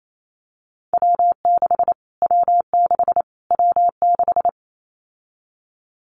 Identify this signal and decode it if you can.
Morse code, heard as a keyed tone, about 28 wpm, 700 Hz. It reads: W6 W6 W6
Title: Callsign Prefixes for Top 25 DXCC - Repeated 3x - 28wpm